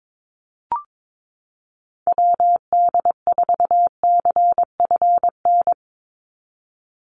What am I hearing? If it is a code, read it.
Morse code, heard as a keyed tone, about 22 wpm, 700 Hz. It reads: WD4CFN
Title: NA SOTA Chaser Callsigns 22wpm